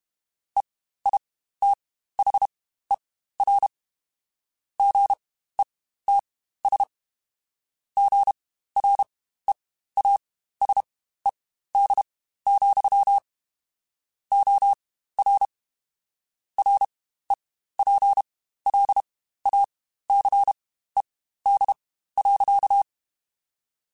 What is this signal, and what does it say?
Morse code, heard as a keyed tone, about 32 wpm, spaced as for 15 wpm, 800 Hz. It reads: EITHER GETS GREASED, OR REPLACED.
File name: Spaced / Chuckles Spaced-mp3s / Squeaky sp @30